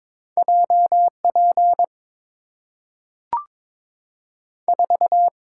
Morse code, heard as a keyed tone, about 22 wpm, 700 Hz. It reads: J P 4
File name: practice / 22wpm / Two Letter-Number Combinations - 22wpm